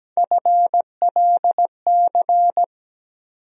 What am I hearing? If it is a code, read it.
Morse code, heard as a keyed tone, about 17 wpm, 700 Hz. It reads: FLC
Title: USA Call Sign Postfixes - 3 Letters 17wpm